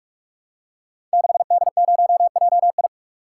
Morse code, heard as a keyed tone, about 45 wpm, 700 Hz. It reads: 6D0JI
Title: Code Groups - 5 Letter:Numbers - 45wpm